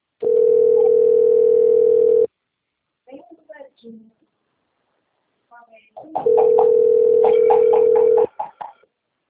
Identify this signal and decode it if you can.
Ringback tone (North America), 440 + 480 Hz, with speech.
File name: rg-1600-unknown- - - .10695